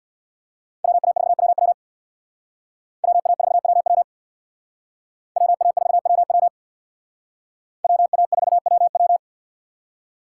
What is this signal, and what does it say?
Morse code, heard as a keyed tone, about 64 wpm, 700 Hz. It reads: WA4WW WA4WW WA4WW WA4WW